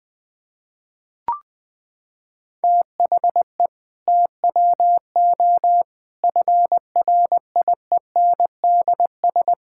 Morse code, heard as a keyed tone, about 20 wpm, 700 Hz. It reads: THE TWO FRIENDS
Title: Sentences with 2-Word Phrases - 20wpm